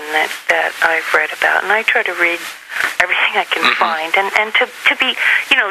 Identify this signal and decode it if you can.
Emergency Alert System (SAME) tones (header bursts), heard as harsh buzzing bursts, with speech. It is heard from 0.7 to 4.9 s.